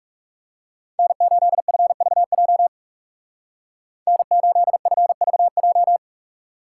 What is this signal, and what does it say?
Morse code, heard as a keyed tone, about 37 wpm, 700 Hz. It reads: N8FVJ N8FVJ
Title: Speed Racing - USA Call Signs 30wpm